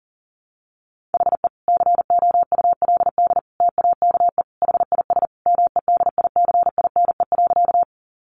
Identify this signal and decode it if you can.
Morse code, heard as a keyed tone, about 40 wpm, 700 Hz. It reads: HE COULD TAKE HIS MEDICINE.
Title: Sentences from Top 900 Words 40wpm